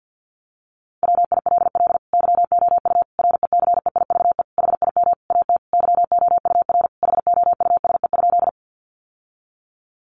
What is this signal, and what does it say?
Morse code, heard as a keyed tone, about 50 wpm, 700 Hz. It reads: WILL YOU RECEIVE HIM AT YOUR HOUSE?
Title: Sentences from Top 1000 Words 50wpm